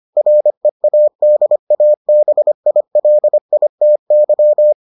Morse code, heard as a keyed tone, about 25 wpm, 600 Hz. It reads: READABILITY